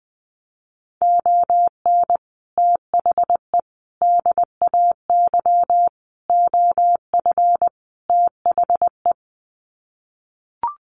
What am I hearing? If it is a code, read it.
Morse code, heard as a keyed tone, about 20 wpm, 700 Hz. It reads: ON THE DAY OF THE
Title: Sets of 5 Words Encore 20wpm